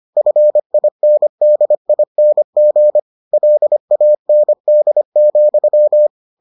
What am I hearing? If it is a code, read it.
Morse code, heard as a keyed tone, about 25 wpm, 600 Hz. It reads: FINDING LAND,